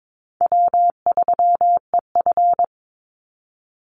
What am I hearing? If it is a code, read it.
Morse code, heard as a keyed tone, about 22 wpm, 690 Hz. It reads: W3EF
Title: Easy Call Signs Repeated 2x - 22wpm